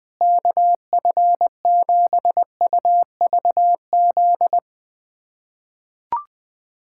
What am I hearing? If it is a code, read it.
Morse code, heard as a keyed tone, about 20 wpm, 700 Hz. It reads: KF7UVZ
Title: USA Callsign DE Callsign - 6 Characters - 20wpm